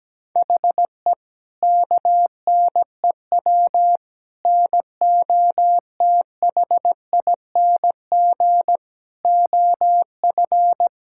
Morse code, heard as a keyed tone, about 17 wpm, 700 Hz. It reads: HE KNEW NOTHING OF
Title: Sentences from Top 1600 Words 17wpm